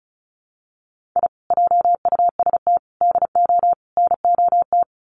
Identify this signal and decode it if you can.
Morse code, heard as a keyed tone, about 34 wpm, 700 Hz. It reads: I JUST DO NOT
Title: Sets of 4 Words 35wpm